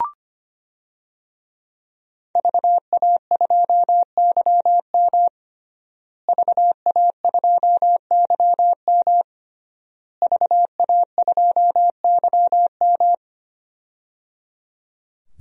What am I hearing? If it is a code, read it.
Morse code, heard as a keyed tone, about 25 wpm, 700 Hz. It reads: VA2YM VA2YM VA2YM
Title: DXCC-100 Call Signs - 5 Characters - Repeated 3x - 25wpm